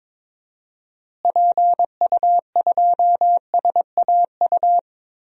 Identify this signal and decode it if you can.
Morse code, heard as a keyed tone, about 22 wpm, 700 Hz. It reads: PU2SAU
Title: DXCC-100 Call Signs - 6 Characters - Repeated 2x - 22wpm